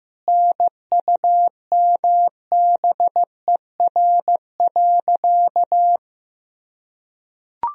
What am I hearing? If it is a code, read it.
Morse code, heard as a keyed tone, about 15 wpm, 700 Hz. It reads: NUMBER.
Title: Sentences from Top 1000 Words - Review Part 4 - 15wpm